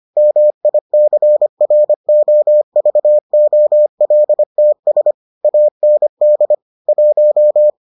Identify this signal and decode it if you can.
Morse code, heard as a keyed tone, about 25 wpm, 600 Hz. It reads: MICROVOLTS AND 1